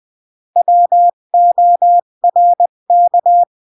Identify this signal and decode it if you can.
Morse code, heard as a keyed tone, about 20 wpm, 700 Hz. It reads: WORK
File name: CW / jodorowsky / 82 commandments 20-15